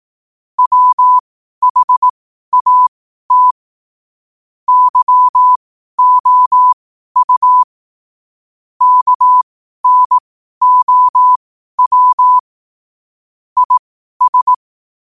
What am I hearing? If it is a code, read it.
Morse code, heard as a keyed tone, about 18 wpm, spaced as for 12 wpm, 1000 Hz. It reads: WHAT YOU KNOW IS
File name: qotdfiles / 12wpm